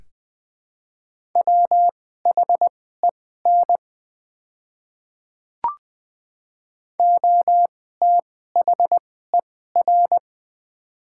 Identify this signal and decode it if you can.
Morse code, heard as a keyed tone, about 20 wpm, spaced as for 14 wpm, 700 Hz. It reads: WHEN OTHER
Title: 2x Character Spacing - Top 100 Words - 20wpm